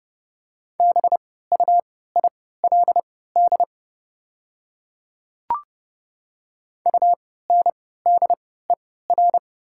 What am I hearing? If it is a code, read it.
Morse code, heard as a keyed tone, about 30 wpm, spaced as for 17 wpm, 700 Hz. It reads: BUILD UNDER
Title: 3x Character Spacing - Top 200 Words - 30wpm